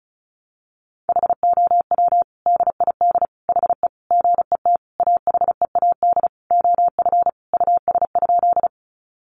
Morse code, heard as a keyed tone, about 34 wpm, 700 Hz. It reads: HOW DID HE GET AHEAD OF US?
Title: Sentences from Top 1100 Words 35wpm